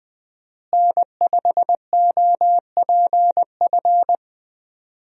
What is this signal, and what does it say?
Morse code, heard as a keyed tone, about 20 wpm, 700 Hz. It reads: N5OPF
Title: USA Call Signs - 5 Characters - Repeated 2x - 20wpm